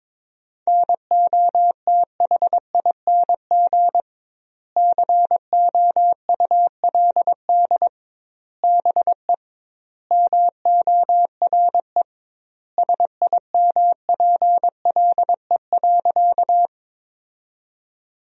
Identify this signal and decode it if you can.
Morse code, heard as a keyed tone, about 22 wpm, 700 Hz. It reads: NOTHING COULD BE MORE SIMPLE.
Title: 2x Word-Spacing - Sentences from Top 500 Words - 22wpm